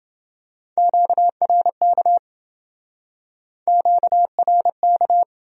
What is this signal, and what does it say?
Morse code, heard as a keyed tone, about 27 wpm, 700 Hz. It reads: QRK QRK